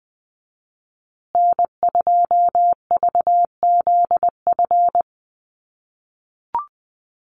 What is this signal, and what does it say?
Morse code, heard as a keyed tone, about 20 wpm, 700 Hz. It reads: N2VZF
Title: USA Call Signs Repeated 3x 20wpm